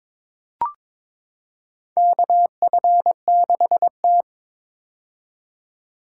Morse code, heard as a keyed tone, about 22 wpm, 700 Hz. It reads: KF6T
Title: K1USN SST QSO Practice - 22wpm